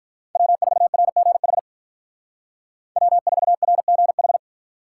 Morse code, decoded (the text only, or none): W4RGH W4RGH